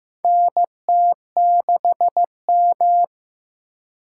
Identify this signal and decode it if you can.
Morse code, heard as a keyed tone, about 15 wpm, 700 Hz. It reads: NT6M
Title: USA Call Signs - 4 Characters - 4x Character Spacing - 15wpm